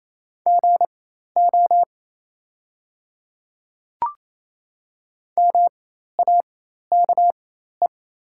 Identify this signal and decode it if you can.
Morse code, heard as a keyed tone, about 28 wpm, spaced as for 13 wpm, 690 Hz. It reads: GO MAKE
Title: 4x Character Spacing - Top 100 Words - 28wpm